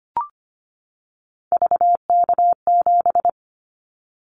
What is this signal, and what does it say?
Morse code, heard as a keyed tone, about 25 wpm, 690 Hz. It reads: VK7
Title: Callsign Prefixes for Top 50 DXCC - 25wpm